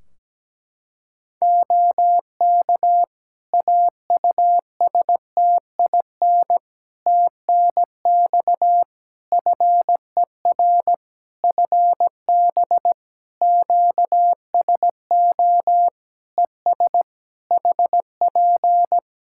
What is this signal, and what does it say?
Morse code, heard as a keyed tone, about 17 wpm, 700 Hz. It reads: OK AUSTIN TNX FER FB QSO ES HP
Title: Practice QSO - Exchange 4 - Ending - 17wpm